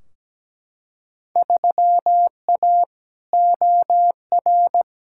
Morse code, heard as a keyed tone, about 17 wpm, 700 Hz. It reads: 3A OR
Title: Field Day Exchanges - 17wpm